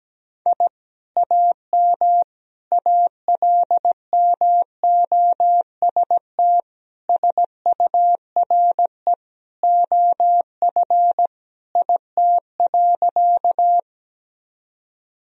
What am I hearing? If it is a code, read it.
Morse code, heard as a keyed tone, about 17 wpm, 700 Hz. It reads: I AM ALMOST SURE OF IT.
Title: Sentences from Top 1000 Words - Review Part 4 - 17wpm